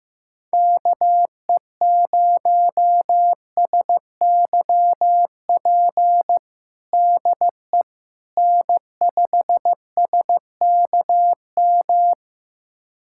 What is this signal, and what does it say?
Morse code, heard as a keyed tone, about 15 wpm, 700 Hz. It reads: KE0SYP DE N5SKM
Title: USA Callsign DE Callsign 15wpm